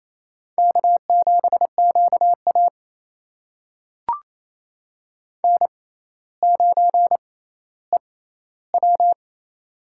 Morse code, heard as a keyed tone, about 29 wpm, spaced as for 15 wpm, 700 Hz. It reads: K7QA N9EW
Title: USA Call Signs - 4 Characters - 6x Character Spacing - 28wpm